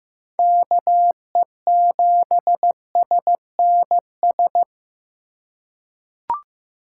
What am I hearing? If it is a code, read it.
Morse code, heard as a keyed tone, about 15 wpm, 700 Hz. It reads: KE7SNS
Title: USA Call Signs - 6 Characters - 3x Character Spacing - 15wpm